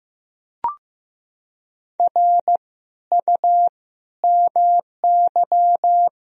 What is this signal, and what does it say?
Morse code, heard as a keyed tone, about 15 wpm, 700 Hz. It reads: R U MY
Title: Sentences from Top 300 Words and Abbreviations 15wpm